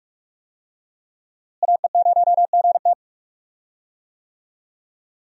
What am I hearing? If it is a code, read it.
Morse code, heard as a keyed tone, about 43 wpm, 700 Hz. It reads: AE0GT
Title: USA Call Signs - 5 Characters - Repeated 2x - 45wpm